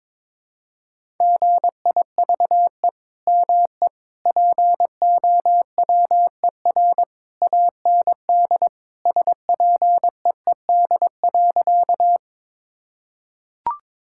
Morse code, heard as a keyed tone, about 22 wpm, 700 Hz. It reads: GIVE ME POWER AND SPEED.
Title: Sentences from Top 1000 Words - Review Part 5 - 22wpm